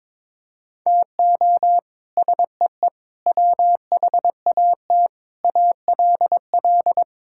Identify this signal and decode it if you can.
Morse code, heard as a keyed tone, about 22 wpm, 700 Hz. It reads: TO SEE WHAT ALL
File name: practice / 22wpm / Sets of 8 Words - 22wpm